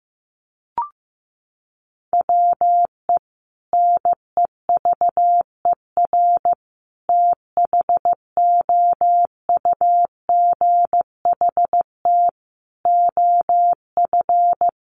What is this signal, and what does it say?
Morse code, heard as a keyed tone, about 15 wpm, 700 Hz. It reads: WE NEVER THOUGHT OF
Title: Sentences from Top 300 Words 15wpm